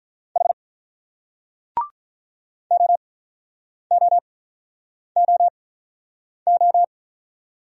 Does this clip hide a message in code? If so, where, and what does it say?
Yes: Morse code from 0.4 to 6.8 s, heard as a keyed tone, about 46 wpm, spaced as for 9 wpm, 690 Hz. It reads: H OOOO